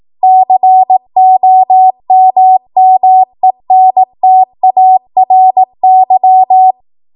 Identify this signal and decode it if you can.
Morse code, heard as a keyed tone, about 18 wpm, 750 Hz. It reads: COMMENTARY
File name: Morse / 18WPM